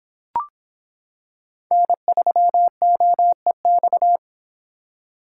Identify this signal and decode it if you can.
Morse code, heard as a keyed tone, about 26 wpm, 700 Hz. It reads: N3OEX